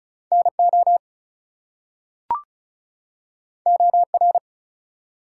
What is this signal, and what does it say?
Morse code, heard as a keyed tone, about 36 wpm, 700 Hz. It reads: NO OR
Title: Two Letter Words - 35wpm